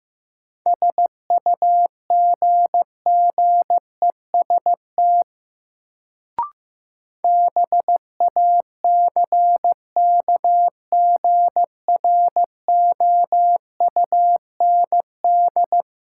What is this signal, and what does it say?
Morse code, heard as a keyed tone, about 15 wpm, 700 Hz. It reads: SUGGEST BACKGROUND
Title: Sentences from Top 1300 Words - New Word Drill - 15wpm